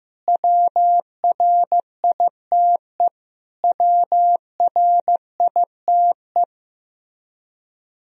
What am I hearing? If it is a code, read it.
Morse code, heard as a keyed tone, about 15 wpm, 700 Hz. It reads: WRITE WRITE
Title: Top 100 Words Repeated 15wpm